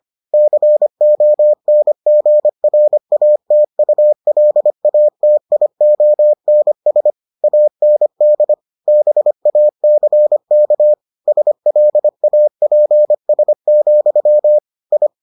Morse code, heard as a keyed tone, about 25 wpm, 600 Hz. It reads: CONGRATULATIONS AND BACK SLAPS, I